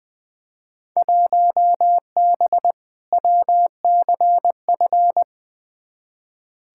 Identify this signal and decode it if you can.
Morse code, heard as a keyed tone, about 20 wpm, 700 Hz. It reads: 1B WCF
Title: Field Day Exchanges Encore - 20wpm